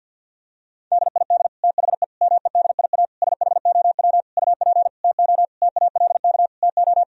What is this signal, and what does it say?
Morse code, heard as a keyed tone, about 50 wpm, 700 Hz. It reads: DID THE MEDIA SHOW UP TO TALK TO
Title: Sentences from Top 1000 Words - Review Part 4 - 50wpm